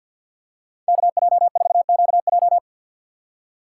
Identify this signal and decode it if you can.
Morse code, heard as a keyed tone, about 47 wpm, 700 Hz. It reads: KJ4QJ
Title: USA Call Signs - 5 Characters - 3x Character Spacing - 50wpm